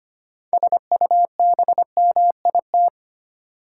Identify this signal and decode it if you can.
Morse code, heard as a keyed tone, about 25 wpm, 700 Hz. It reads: SUBMIT
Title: Six Letter Words - 25wpm